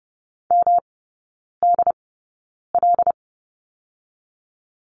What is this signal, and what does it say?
Morse code, heard as a keyed tone, about 30 wpm, spaced as for 9 wpm, 700 Hz. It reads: MDL
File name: Lesson 18e - Letter m - Three Characters 30wpm